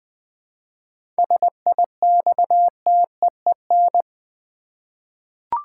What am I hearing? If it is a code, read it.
Morse code, heard as a keyed tone, about 20 wpm, 700 Hz. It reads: SIXTEEN
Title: 7 Letter Words - First 3 Letters Given - 20wpm